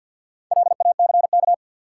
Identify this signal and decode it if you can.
Morse code, heard as a keyed tone, about 52 wpm, 700 Hz. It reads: LACK
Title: Sets of 3 Words - New Word Drill - 50wpm